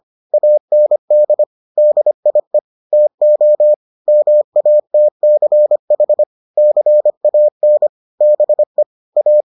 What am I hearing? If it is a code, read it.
Morse code, heard as a keyed tone, about 25 wpm, 600 Hz. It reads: AND DIE TO MATCH CAN BE A